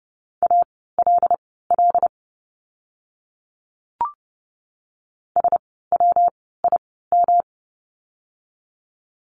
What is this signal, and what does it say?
Morse code, heard as a keyed tone, about 30 wpm, spaced as for 17 wpm, 700 Hz. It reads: ALL SWIM
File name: course / 30:10wpm / 115 Lesson 19f - Letter w - Words 30wpm